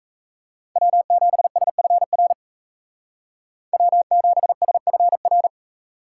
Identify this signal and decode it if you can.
Morse code, heard as a keyed tone, about 36 wpm, 700 Hz. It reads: W7SFR W7SFR